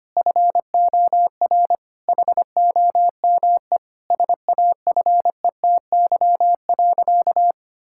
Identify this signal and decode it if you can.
Morse code, heard as a keyed tone, about 25 wpm, 700 Hz. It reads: FOR HOME SAFETY.
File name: practice / 25wpm / Difficult Sentences with Hints - 25wpm